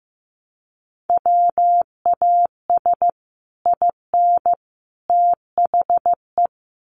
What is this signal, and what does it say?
Morse code, heard as a keyed tone, about 15 wpm, 700 Hz. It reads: WAS IN THE
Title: Sets of 3 Words 15wpm